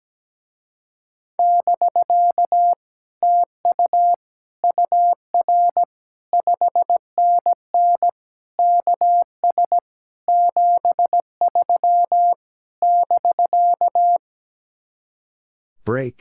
Morse code, heard as a keyed tone, about 17 wpm, 700 Hz. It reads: <BK> TU UR 5NN KS 73 <BK>
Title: Speed-Racing Common QSO Phrases - 17wpm